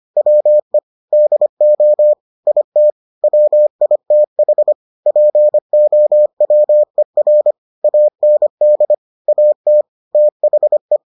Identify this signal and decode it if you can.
Morse code, heard as a keyed tone, about 25 wpm, 600 Hz. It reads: WE DO IT WITH POWER AND AT THE